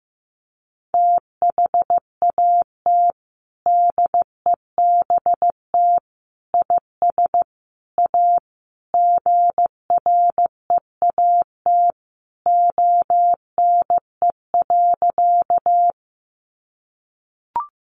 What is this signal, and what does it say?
Morse code, heard as a keyed tone, about 15 wpm, 700 Hz. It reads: THAT DEBT IS A GREAT ONE.
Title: Sentences from Top 1300 Words 15wpm